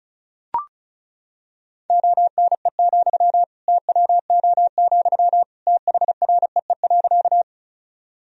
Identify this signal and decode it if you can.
Morse code, heard as a keyed tone, about 36 wpm, 700 Hz. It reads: ONE, TWO, THREE.
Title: Sentences from Top 1000 Words - Review Part 1 - 35wpm